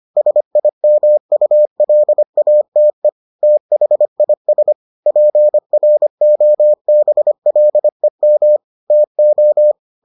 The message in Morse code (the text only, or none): SIMULATE THIS PROBLEM TO